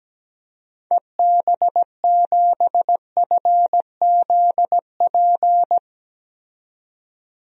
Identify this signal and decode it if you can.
Morse code, heard as a keyed tone, about 17 wpm, 700 Hz. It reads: EB7FZP
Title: DXCC-100 Call Signs - 6 Characters - 17wpm